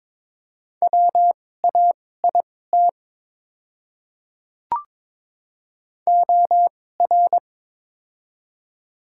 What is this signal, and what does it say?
Morse code, heard as a keyed tone, about 22 wpm, spaced as for 16 wpm, 700 Hz. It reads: WAIT OR
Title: 2x Character Spacing - Top 500 Words - 22wpm